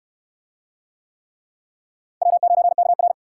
Morse code, heard as a keyed tone, about 67 wpm, 700 Hz. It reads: W9KR